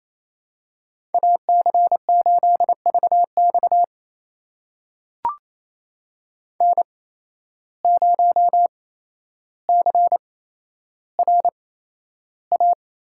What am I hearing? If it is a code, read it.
Morse code, heard as a keyed tone, about 28 wpm, spaced as for 12 wpm, 700 Hz. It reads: AC8VX N0CRA
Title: USA Call Signs - 5 Characters - 8x Character Spacing - 28wpm